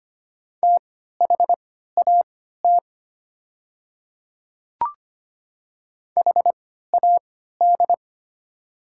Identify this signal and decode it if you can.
Morse code, heard as a keyed tone, about 25 wpm, spaced as for 14 wpm, 700 Hz. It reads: THAT HAD
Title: 3x Character Spacing - Top 100 Words - 25wpm